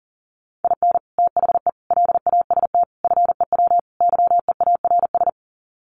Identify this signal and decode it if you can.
Morse code, heard as a keyed tone, about 40 wpm, 700 Hz. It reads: IN THE LAST FEW YEARS
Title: Sets of 5 Words 40wpm